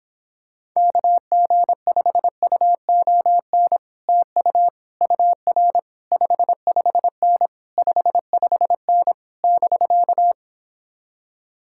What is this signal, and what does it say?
Morse code, heard as a keyed tone, about 26 wpm, 700 Hz. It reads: KG5UON TU UR 55N 55N <BK>